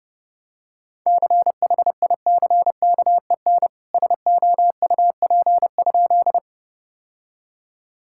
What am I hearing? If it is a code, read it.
Morse code, heard as a keyed tone, about 30 wpm, 700 Hz. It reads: CHICKEN SOUP?
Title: Sentences from Top 1000 Words - Review Part 6 - 30wpm